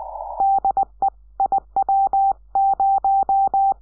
Morse code, heard as a keyed tone, about 19 wpm, 780 Hz. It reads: DE IW0
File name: iw0ffk / 2006 06 12 un7qx iw0ffk